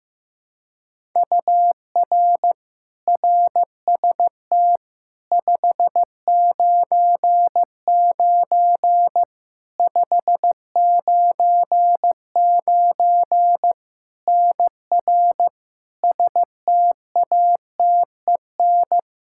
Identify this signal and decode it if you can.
Morse code, heard as a keyed tone, about 15 wpm, 700 Hz. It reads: UR RST 599 599 NR STATEN
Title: Practice QSO - Exchange 1 - Beginning - 15wpm